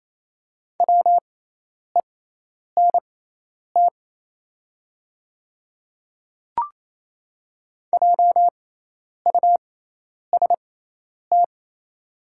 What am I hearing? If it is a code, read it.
Morse code, heard as a keyed tone, about 28 wpm, spaced as for 10 wpm, 700 Hz. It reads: WENT JUST
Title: 6x Character Spacing - Top 300 Words - 28wpm